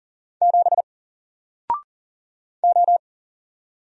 Morse code, heard as a keyed tone, about 40 wpm, 700 Hz. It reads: 7 O
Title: Single Letter-Number Alt 40wpm